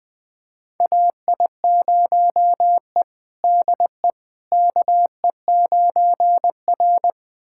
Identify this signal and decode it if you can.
Morse code, heard as a keyed tone, about 20 wpm, 700 Hz. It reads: AI0E DE KE9R
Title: USA Callsign DE Callsign - 4 Characters - 20wpm